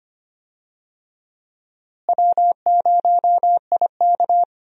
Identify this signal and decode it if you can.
Morse code, heard as a keyed tone, about 25 wpm, 700 Hz. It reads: W0IK
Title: USA Call Signs - 4 Characters - Repeated 3x - 25wpm